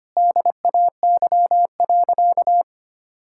Morse code, heard as a keyed tone, about 25 wpm, 700 Hz. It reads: DAY.
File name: practice / 25wpm / Sentences with 2-Word Phrases - 25wpm